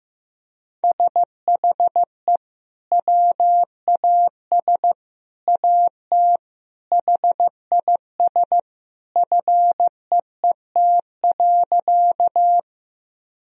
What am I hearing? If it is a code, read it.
Morse code, heard as a keyed tone, about 15 wpm, 700 Hz. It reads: SHE WAS AT HIS FEET.
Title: Sentences from Top 1000 Words - Review Part 1 - 15wpm